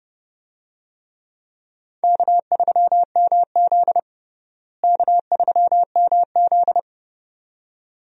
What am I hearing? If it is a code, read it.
Morse code, heard as a keyed tone, about 30 wpm, 700 Hz. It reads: K3MZ K3MZ
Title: USA Call Signs - 4 Characters - Repeated 2x - 30wpm